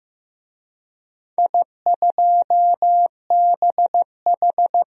Morse code, heard as a keyed tone, about 15 wpm, 700 Hz. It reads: I2BH
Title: DXCC-100 Call Signs - 4 Characters - 15wpm